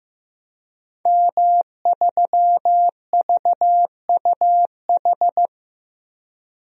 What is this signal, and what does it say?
Morse code, heard as a keyed tone, about 15 wpm, 700 Hz. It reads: M3VUH
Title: DXCC-100 Call Signs - 5 Characters - 15wpm